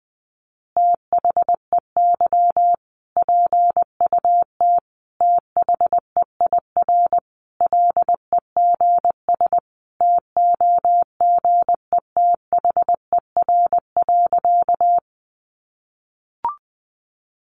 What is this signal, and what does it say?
Morse code, heard as a keyed tone, about 20 wpm, 700 Hz. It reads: THEY PUT THEIR LEGS TOGETHER.
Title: Easy Phrases 20wpm